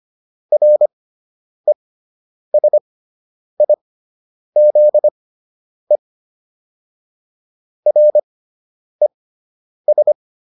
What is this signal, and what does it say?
Morse code, heard as a keyed tone, about 25 wpm, spaced as for 9 wpm, 600 Hz. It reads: RESIZE RES